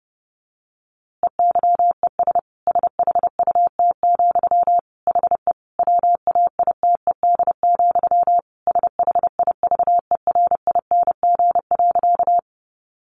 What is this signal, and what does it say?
Morse code, heard as a keyed tone, about 30 wpm, 700 Hz. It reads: EYES SHUT, HE WAITED, SHIVERING.
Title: A Sound Of Thunder 30wpm